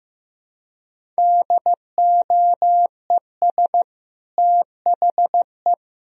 Morse code, heard as a keyed tone, about 15 wpm, 700 Hz. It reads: DOES THE
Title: Sentences from Top 1000 Words - Review Part 4 - 15wpm